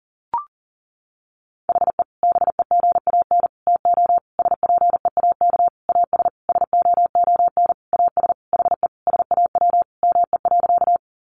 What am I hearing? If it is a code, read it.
Morse code, heard as a keyed tone, about 40 wpm, 700 Hz. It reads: HE BEGAN TO SPEAK AS SOON AS HE SAW ME.
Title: Sentences from Top 1100 Words 40wpm